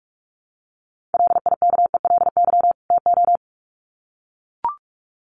Morse code, heard as a keyed tone, about 44 wpm, 700 Hz. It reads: LIKELY TO
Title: Sets of 2 Words 45wpm